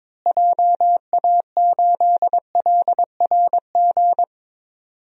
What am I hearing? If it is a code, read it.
Morse code, heard as a keyed tone, about 22 wpm, 700 Hz. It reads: JA8LRG
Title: DXCC-100 Call Signs - 6 Characters - Repeated 3x - 22wpm